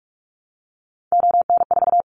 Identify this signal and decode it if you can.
Morse code, heard as a keyed tone, about 46 wpm, 700 Hz. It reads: ON4
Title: Callsign Prefixes for Top 25 DXCC - 45wpm